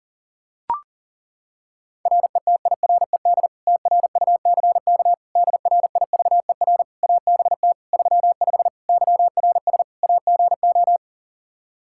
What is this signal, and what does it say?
Morse code, heard as a keyed tone, about 37 wpm, 700 Hz. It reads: RETIRED TRUCK DRIVER ABT 35 YRS AGO
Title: Practice QSO - Exchange 3 - More Info - 40wpm